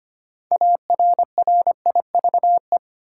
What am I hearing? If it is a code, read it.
Morse code, heard as a keyed tone, about 25 wpm, 700 Hz. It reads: ARRIVE
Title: Six Letter Words - 25wpm